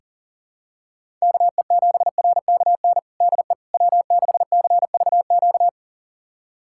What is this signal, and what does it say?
Morse code, heard as a keyed tone, about 40 wpm, 700 Hz. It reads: KE7RKN DE W6CVQ
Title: USA Callsign DE Callsign 40wpm